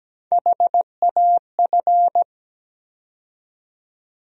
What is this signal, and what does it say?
Morse code, heard as a keyed tone, about 17 wpm, 700 Hz. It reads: HAF
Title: USA Call Sign Postfixes - 3 Letters 17wpm